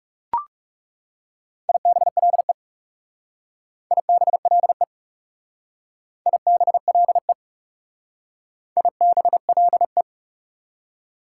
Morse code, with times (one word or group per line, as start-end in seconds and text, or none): 1.7-2.5 IBLE
3.9-4.8 IBLE
6.3-7.3 IBLE
8.8-10.0 IBLE